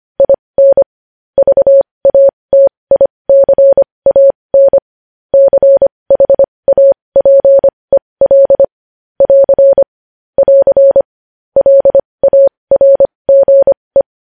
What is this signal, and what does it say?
Morse code, heard as a keyed tone, about 25 wpm, spaced as for 20 wpm, 550 Hz. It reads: IN VATICAN CHAPEL <AR> <AR> LARGE
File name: podcast / News headlines-1x-25x20